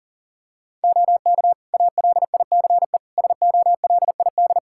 Morse code, heard as a keyed tone, about 40 wpm, 700 Hz. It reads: OK ALICE SOLID
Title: Practice QSO - Exchange 3 - More Info - 40wpm